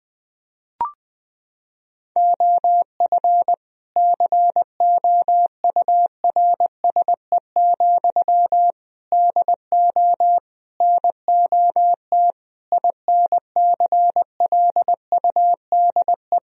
Morse code, heard as a keyed tone, about 21 wpm, 700 Hz. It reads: OF COURSE, DO NOT INCLUDE
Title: Sentences from Top 1000 Words - Review Part 3 - 20wpm